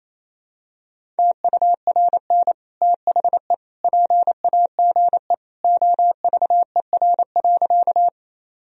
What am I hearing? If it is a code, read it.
Morse code, heard as a keyed tone, about 28 wpm, 700 Hz. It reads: TURN THE PAGE OVER.
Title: Sentences from Top 1000 Words - Review Part 1 - 28wpm